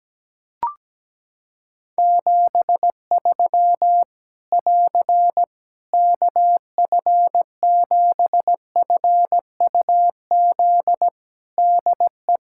Morse code, heard as a keyed tone, about 17 wpm, 700 Hz. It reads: 73 <AR> KF7FUZ DE
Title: Practice QSO - Exchange 4 - Ending - 17wpm